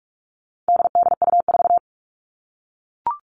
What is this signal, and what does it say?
Morse code, heard as a keyed tone, about 44 wpm, 700 Hz. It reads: DDU4
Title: All Cut Numbers - 4-Digit 45wpm